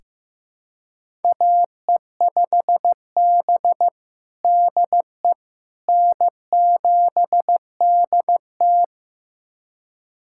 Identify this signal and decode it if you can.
Morse code, heard as a keyed tone, about 15 wpm, 700 Hz. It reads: AE5B DE N7DT
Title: USA Callsign DE Callsign - 4 Characters - 15wpm